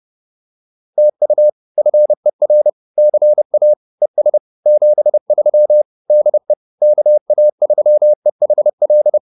Morse code, heard as a keyed tone, about 30 wpm, 600 Hz. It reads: TU FER CA ES 73 DE KA3EHL
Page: advanced-practice-files